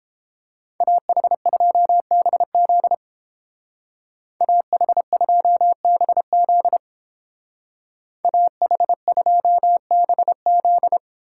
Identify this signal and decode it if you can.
Morse code, heard as a keyed tone, about 28 wpm, 700 Hz. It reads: AH2BZ AH2BZ AH2BZ